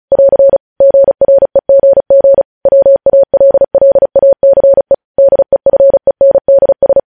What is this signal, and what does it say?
Morse code, heard as a keyed tone, about 36 wpm, 550 Hz. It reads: <AR> GREGG WALLACE DEFENDS
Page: News headlines in Morse code at 35 wpm